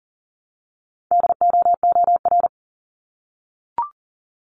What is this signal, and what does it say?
Morse code, heard as a keyed tone, about 41 wpm, 700 Hz. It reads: DOOR
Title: Multi-letter Phonemes 40wpm